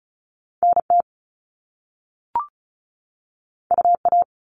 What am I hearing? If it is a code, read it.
Morse code, heard as a keyed tone, about 36 wpm, 700 Hz. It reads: NT UA
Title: Top Two Letter Combinations in Words 35wpm